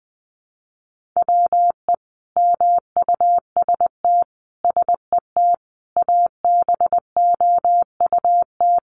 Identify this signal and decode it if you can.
Morse code, heard as a keyed tone, about 20 wpm, 700 Hz. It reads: WE MUST SET ABOUT
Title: Sentences from Top 1400 Words 20wpm